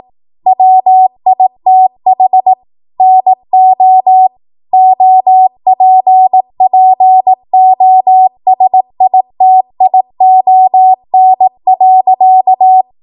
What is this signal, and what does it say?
Morse code, heard as a keyed tone, about 18 wpm, 750 Hz. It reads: WITH NO OPPOSITION.
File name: Morse / 18WPM